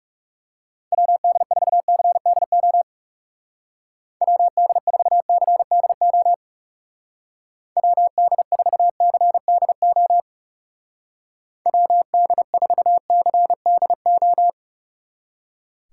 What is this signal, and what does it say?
Morse code, heard as a keyed tone, about 35 wpm, 700 Hz. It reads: WD4CDO WD4CDO WD4CDO WD4CDO
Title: Speed Racing - USA 2x3 Call Signs 30wpm